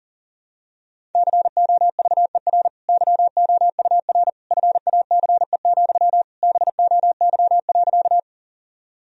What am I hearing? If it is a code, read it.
Morse code, heard as a keyed tone, about 40 wpm, 700 Hz. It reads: COVER YOUR FACE, BOY.
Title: Sentences from Top 1000 Words - Review Part 1 - 40wpm